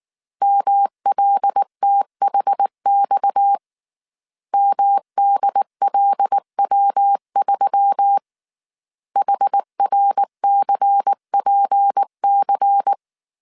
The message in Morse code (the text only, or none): MLTHX MDLW3 HRCPC